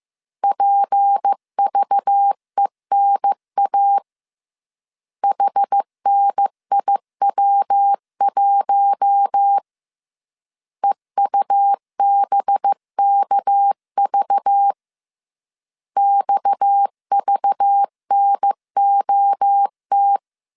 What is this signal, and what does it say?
Morse code, heard as a keyed tone, about 15 wpm, 800 Hz. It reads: PVENA HNIW1 EUBKV XVNOT